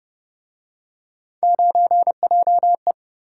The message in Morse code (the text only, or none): 9JE